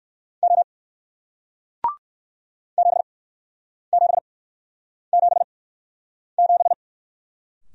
Morse code, heard as a keyed tone, about 59 wpm, spaced as for 9 wpm, 700 Hz. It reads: O 7777